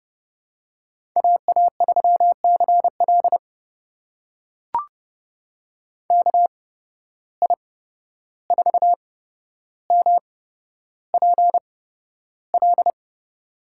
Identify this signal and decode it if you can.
Morse code, heard as a keyed tone, about 31 wpm, spaced as for 12 wpm, 700 Hz. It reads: AA3CL KI4MPL
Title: USA Call Signs - 8x Character Spacing - 30wpm